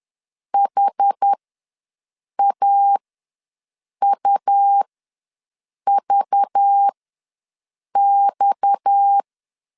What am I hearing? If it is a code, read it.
Morse code, heard as a keyed tone, about 10 wpm, spaced as for 6 wpm, 800 Hz. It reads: HAUVX
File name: LEZIONI / 6 PKX0 / 6 PKX ws 10wpm 3706